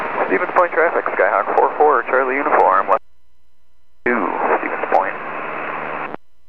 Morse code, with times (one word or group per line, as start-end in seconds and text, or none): none